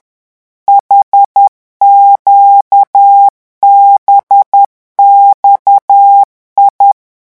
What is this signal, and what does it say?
Morse code, heard as a keyed tone, about 10 wpm, 800 Hz. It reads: HQBXI
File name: morse mp3 / revision8-vitesse 10 mots